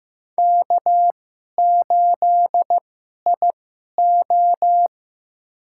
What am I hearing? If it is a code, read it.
Morse code, heard as a keyed tone, about 15 wpm, spaced as for 11 wpm, 700 Hz. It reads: K8IO
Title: USA Call Signs - 4 Characters - 2x Character Spacing - 15wpm